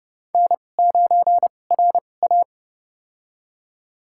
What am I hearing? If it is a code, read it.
Morse code, heard as a keyed tone, about 30 wpm, spaced as for 22 wpm, 700 Hz. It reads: N9RA
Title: USA Call Signs - 4 Characters - 2x Character Spacing - 30wpm